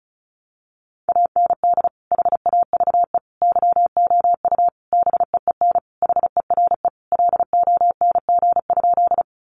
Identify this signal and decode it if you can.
Morse code, heard as a keyed tone, about 34 wpm, 700 Hz. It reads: AND HAVE YOU BEEN HERE LONG?
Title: Sentences from Top 200 Words 35wpm